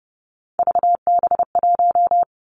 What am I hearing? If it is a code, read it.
Morse code, heard as a keyed tone, about 30 wpm, 700 Hz. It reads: VB1
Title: Three Challenging Letter-Number Combinations 30wpm